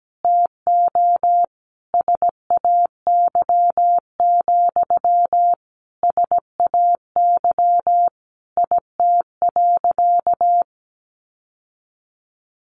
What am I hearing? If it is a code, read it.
Morse code, heard as a keyed tone, about 17 wpm, 700 Hz. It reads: TO SAY, SAY IT.
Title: Sentences from Top 1000 Words 17wpm